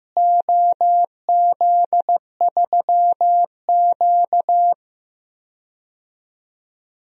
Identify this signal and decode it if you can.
Morse code, heard as a keyed tone, about 15 wpm, 700 Hz. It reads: OZ3Q
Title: DXCC-100 Call Signs - 4 Characters - Repeated 3x - 15wpm